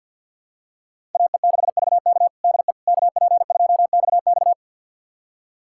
Morse code, heard as a keyed tone, about 51 wpm, 700 Hz. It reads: AE6VK DE KP2XX
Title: USA Callsign DE Callsign - 5 Characters - 50wpm